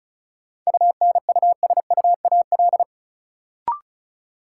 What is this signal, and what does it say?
Morse code, heard as a keyed tone, about 34 wpm, 700 Hz. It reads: UNUSUAL
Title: Sentences from Top 1400 Words - New Word Drill Alt - 35wpm